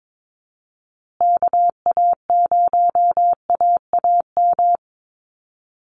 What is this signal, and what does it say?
Morse code, heard as a keyed tone, about 22 wpm, 700 Hz. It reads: KA0AAM
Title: Call Signs Repeated 2x - 22wpm